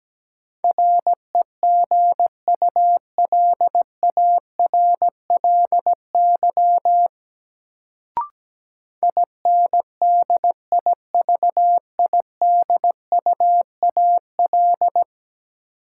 Morse code, heard as a keyed tone, about 17 wpm, 700 Hz. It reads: REGULARLY INDIVIDUAL
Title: Sentences from Top 1400 Words - New Word Drill Alt - 17wpm